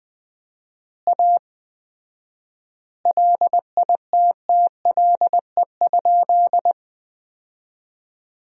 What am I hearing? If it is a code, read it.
Morse code, heard as a keyed tone, about 21 wpm, 700 Hz. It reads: A LITTLE?
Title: 4x Word-Spacing - Sentences from Top 500 Words - 20wpm